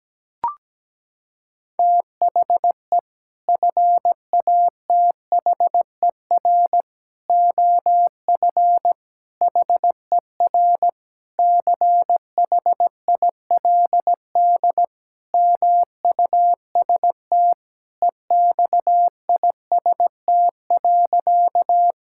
Morse code, heard as a keyed tone, about 17 wpm, 700 Hz. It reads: THE FATHER OF HER CHILD MUST EXIST.
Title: Sentences from Top 1000 Words - Review Part 5 - 17wpm